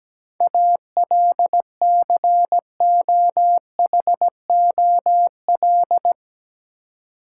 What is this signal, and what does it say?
Morse code, heard as a keyed tone, about 17 wpm, 700 Hz. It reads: ALCOHOL
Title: Seven Letter Words - 17wpm